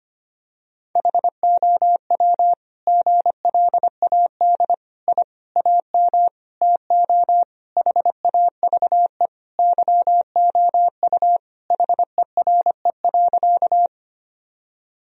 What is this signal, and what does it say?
Morse code, heard as a keyed tone, about 25 wpm, 700 Hz. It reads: HOW GLAD I AM TO HAVE YOU HERE.
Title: Sentences from Top 1000 Words - Review Part 2 - 25wpm